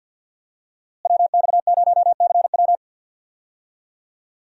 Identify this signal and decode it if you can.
Morse code, heard as a keyed tone, about 51 wpm, 700 Hz. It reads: WX0CW
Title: USA Call Signs - 5 Characters - Repeated 3x - 50wpm